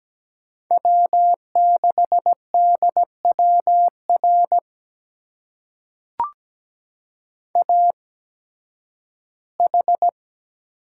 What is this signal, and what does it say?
Morse code, heard as a keyed tone, about 17 wpm, spaced as for 11 wpm, 700 Hz. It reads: W6DWR AH
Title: USA Call Signs - 8x Character Spacing - 17wpm